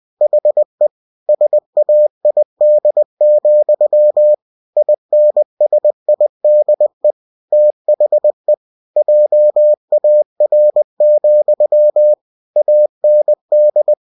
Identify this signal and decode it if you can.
Morse code, heard as a keyed tone, about 20 wpm, 600 Hz. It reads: HE SAID, INSIDE THE JAR, AND
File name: cw-books / winnie-the-pooh / winnie-20wpm-page-0042